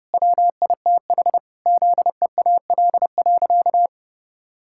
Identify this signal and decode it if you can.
Morse code, heard as a keyed tone, about 30 wpm, 700 Hz. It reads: WITH ZEAL.
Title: Pangram Challenge v2 - 30wpm